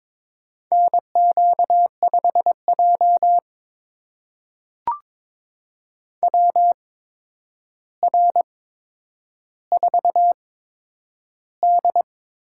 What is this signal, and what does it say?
Morse code, heard as a keyed tone, about 22 wpm, spaced as for 9 wpm, 700 Hz. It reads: NQ5J WR4D
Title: USA Call Signs - 4 Characters - 8x Character Spacing - 22wpm